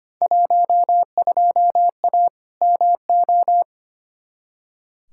Morse code, heard as a keyed tone, about 25 wpm, 700 Hz. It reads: 12A MO
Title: Field Day Exchanges Encore - 25wpm